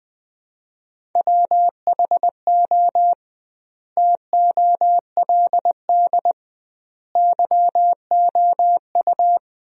Morse code, heard as a keyed tone, about 20 wpm, 700 Hz. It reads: WHO TOLD YOU
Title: 2x Word-Spacing - Sentences from Top 400 Words - 20wpm